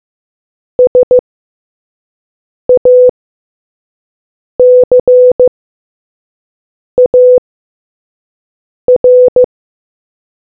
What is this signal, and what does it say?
Morse code, heard as a keyed tone, about 15 wpm, spaced as for 5 wpm, 500 Hz. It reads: SACAR